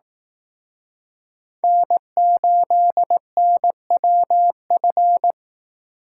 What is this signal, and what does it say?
Morse code, heard as a keyed tone, about 19 wpm, 700 Hz. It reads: N8NWF
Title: Speed Racing - USA 1x3 Call Signs 15wpm